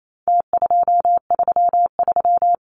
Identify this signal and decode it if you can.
Morse code, heard as a keyed tone, about 28 wpm, 700 Hz. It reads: T233
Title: Numbers - Contest Style - 4-Digit 28wpm